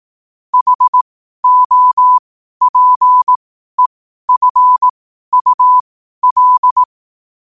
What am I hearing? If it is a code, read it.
Morse code, heard as a keyed tone, about 18 wpm, spaced as for 12 wpm, 1000 Hz. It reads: HOPEFUL
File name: qotdfiles / 12wpm